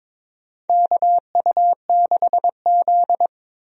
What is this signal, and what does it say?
Morse code, heard as a keyed tone, about 22 wpm, 700 Hz. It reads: KU6Z
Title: USA Call Signs - 4 Characters - Repeated 3x - 22wpm